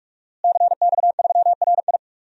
Morse code, heard as a keyed tone, about 46 wpm, 700 Hz. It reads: CX3RI
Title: DXCC-100 Call Signs - 5 Characters - Repeated 3x - 45wpm